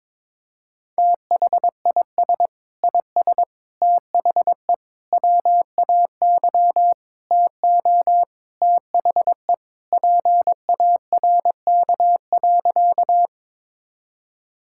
Morse code, heard as a keyed tone, about 22 wpm, 700 Hz. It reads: THIS IS THE WAY TO THE PARK.
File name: practice / 22wpm / Sentences with 4-Word Phrases - 22wpm